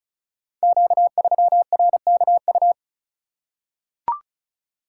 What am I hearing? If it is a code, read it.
Morse code, heard as a keyed tone, about 36 wpm, 690 Hz. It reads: Q3RKU
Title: Code Groups - 5 Letter:Numbers - 35wpm